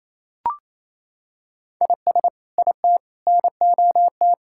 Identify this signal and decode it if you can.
Morse code, heard as a keyed tone, about 28 wpm, 700 Hz. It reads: IS IT NOT
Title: Sentences from Top 1600 Words 28wpm